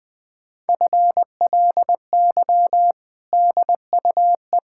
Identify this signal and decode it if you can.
Morse code, heard as a keyed tone, about 20 wpm, 700 Hz. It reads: FLY DUE
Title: Sentences from Top 1000 Words - Review Part 6 - 20wpm